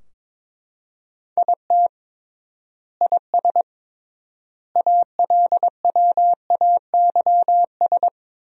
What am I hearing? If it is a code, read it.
Morse code, heard as a keyed tone, about 22 wpm, 700 Hz. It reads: IT IS ALWAYS
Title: 3x Word-Spacing - Sentences from Top 300 Words - 22wpm